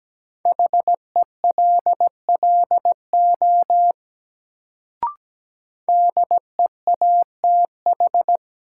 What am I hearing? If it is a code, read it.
Morse code, heard as a keyed tone, about 17 wpm, 700 Hz. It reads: HELLO DEATH
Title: Sentences from Top 1100 Words - New Word Drill - 17wpm